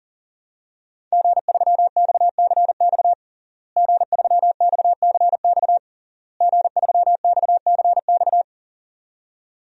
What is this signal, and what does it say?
Morse code, heard as a keyed tone, about 37 wpm, 700 Hz. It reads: G3XCX G3XCX G3XCX
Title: DXCC-100 Call Signs - 5 Characters - Repeated 3x - 40wpm